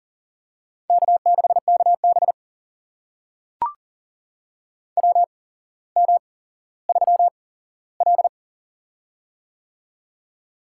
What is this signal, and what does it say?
Morse code, heard as a keyed tone, about 40 wpm, spaced as for 17 wpm, 700 Hz. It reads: K6KB WM3L
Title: USA Call Signs - 4 Characters - 8x Character Spacing - 40wpm